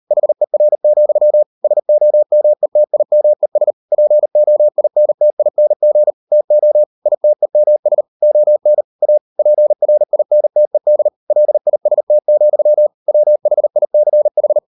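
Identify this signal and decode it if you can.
Morse code, heard as a keyed tone, about 38 wpm, 600 Hz. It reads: HER, SOMETIMES POINTING TO ITEMS ON A PRINTED LIST, WHICH
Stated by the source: Morse code practice at 38 WPM, sending Ken Follett's novel Never in English, from CW med Gnister.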